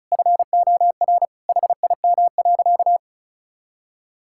Morse code, heard as a keyed tone, about 34 wpm, 700 Hz. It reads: FOR HIM.
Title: Sentences from Top 1600 Words 35wpm